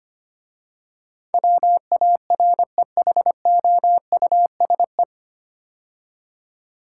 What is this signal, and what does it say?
Morse code, heard as a keyed tone, about 25 wpm, 700 Hz. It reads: WAREHOUSE
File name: practice / 25wpm / 9 Letter Words - First 3 Letters Given - 25wpm